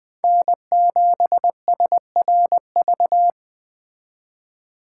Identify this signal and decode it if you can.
Morse code, heard as a keyed tone, about 18 wpm, 700 Hz. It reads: N7SRV
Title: Speed Racing - USA 1x3 Call Signs 17wpm